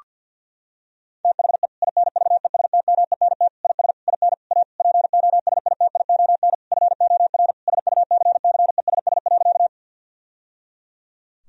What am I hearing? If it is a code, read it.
Morse code, heard as a keyed tone, about 50 wpm, 700 Hz. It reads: THE INVESTMENT IS IN A POSITION FOR SUCCESS.
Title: Sentences from Top 1000 Words - Review Part 4 - 50wpm